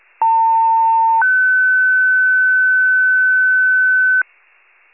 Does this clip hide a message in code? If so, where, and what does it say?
Yes: two-tone page from 0.2 to 4.2 s, heard as a one-second tone, then a three-second tone, about 900 then 1500 Hz.